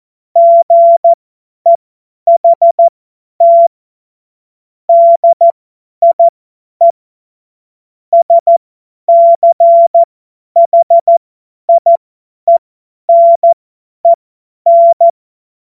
Morse code, heard as a keyed tone, about 14 wpm, spaced as for 10 wpm, 680 Hz. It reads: GEHT DIE SCHIENEN
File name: Morsekurs / Witze / Witz 01. August 2023 50 F70